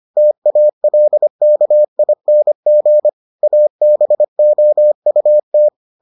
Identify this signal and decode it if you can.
Morse code, heard as a keyed tone, about 25 wpm, 600 Hz. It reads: TALKING ABOUT